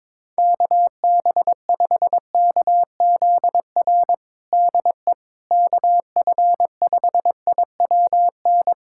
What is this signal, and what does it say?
Morse code, heard as a keyed tone, about 21 wpm, 700 Hz. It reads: KB5KZR DE KF5IWN
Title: USA Callsign DE Callsign - 6 Characters - 22wpm